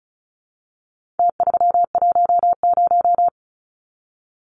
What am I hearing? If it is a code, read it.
Morse code, heard as a keyed tone, about 34 wpm, 700 Hz. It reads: T310
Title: Numbers - Contest Style - 4-Digit 35wpm